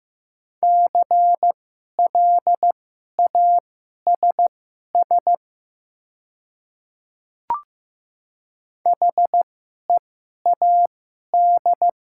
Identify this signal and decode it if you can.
Morse code, heard as a keyed tone, about 15 wpm, spaced as for 11 wpm, 690 Hz. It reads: CLASS HEAD
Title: 2x Character Spacing - Top 500 Words - 15wpm